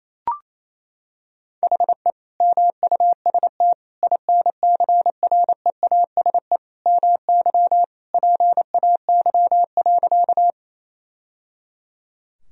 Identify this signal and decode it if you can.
Morse code, heard as a keyed tone, about 28 wpm, 700 Hz. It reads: HE MUST INCREASE MY PAY.
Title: Sentences from Top 1000 Words - Review Part 4 - 28wpm